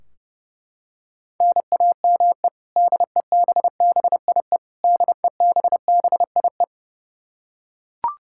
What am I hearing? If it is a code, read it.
Morse code, heard as a keyed tone, about 30 wpm, 700 Hz. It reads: NAME DEBBIE DEBBIE
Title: Practice QSO - Exchange 1 - Beginning - 30wpm